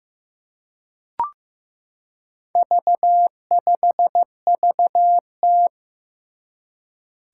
Morse code, heard as a keyed tone, about 15 wpm, 700 Hz. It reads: V5VT
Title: Four Challenging Letter-Number Combinations 15wpm